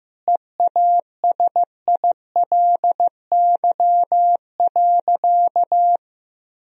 Morse code, heard as a keyed tone, about 15 wpm, 700 Hz. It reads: EASILY.
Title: Sentences from Top 1000 Words - Review Part 1 - 15wpm